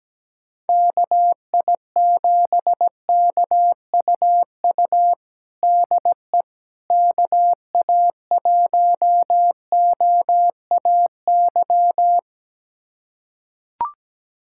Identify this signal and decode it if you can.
Morse code, heard as a keyed tone, about 17 wpm, 700 Hz. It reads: KI7KUU DE KA1OAY
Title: USA Callsign DE Callsign 17wpm